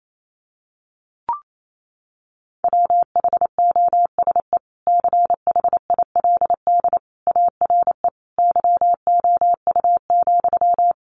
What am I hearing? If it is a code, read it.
Morse code, heard as a keyed tone, about 28 wpm, 700 Hz. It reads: WHOSE CHILD ARE YOU,
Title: Sentences from Top 1200 Words 28wpm